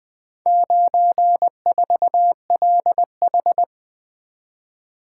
Morse code, heard as a keyed tone, about 20 wpm, 700 Hz. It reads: 94LH
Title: Four Letter-Number Combinations - 20wpm